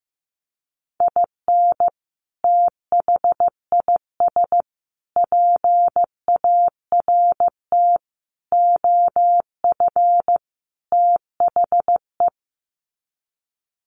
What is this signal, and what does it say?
Morse code, heard as a keyed tone, about 15 wpm, 700 Hz. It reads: IN THIS PART OF THE
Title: Sets of 5 Words 15wpm